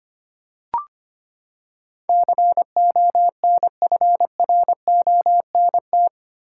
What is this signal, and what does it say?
Morse code, heard as a keyed tone, about 25 wpm, 700 Hz. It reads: CONFRONT
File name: Eight Letter Words - 25wpm